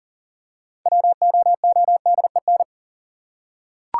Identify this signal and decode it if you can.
Morse code, heard as a keyed tone, about 39 wpm, 700 Hz. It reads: WOODEN
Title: Six Letter Words - 40wpm